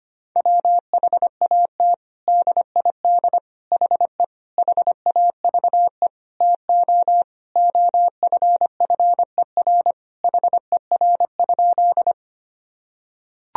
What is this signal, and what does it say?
Morse code, heard as a keyed tone, about 25 wpm, 700 Hz. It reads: WHAT DID HE HAVE TO OFFER HER?
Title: Sentences from Top 1000 Words - Review Part 4 - 25wpm